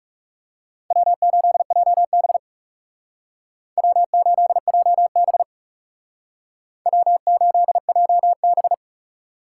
Morse code, heard as a keyed tone, about 38 wpm, 700 Hz. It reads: W8JB W8JB W8JB